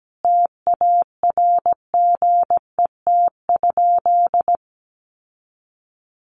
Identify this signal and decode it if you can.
Morse code, heard as a keyed tone, about 17 wpm, 700 Hz. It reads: TARGET?
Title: Sentences from Top 1200 Words 17wpm